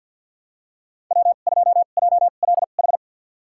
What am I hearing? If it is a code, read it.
Morse code, heard as a keyed tone, about 48 wpm, spaced as for 35 wpm, 700 Hz. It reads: W2JLH
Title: USA Call Signs - 5 Characters - 2x Character Spacing - 50wpm